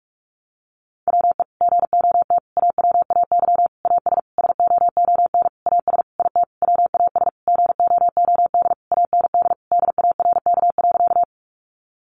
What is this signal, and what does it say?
Morse code, heard as a keyed tone, about 44 wpm, 700 Hz. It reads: WE GOT AWAY AS SOON AS IT WAS GOOD AND DARK.
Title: Sentences from Top 600 Words 45wpm